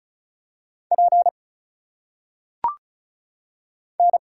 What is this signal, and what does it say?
Morse code, heard as a keyed tone, about 36 wpm, 700 Hz. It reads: P N
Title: Single Letters - Phonetic Pronunciation 35wpm